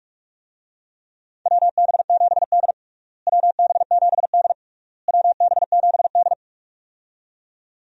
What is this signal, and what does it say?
Morse code, heard as a keyed tone, about 46 wpm, 700 Hz. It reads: WB7D WB7D WB7D
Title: USA Call Signs - 4 Characters - Repeated 3x - 45wpm